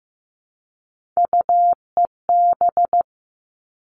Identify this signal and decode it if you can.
Morse code, heard as a keyed tone, about 15 wpm, 700 Hz. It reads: UEB
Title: All Cut Numbers - 3-Digit 15wpm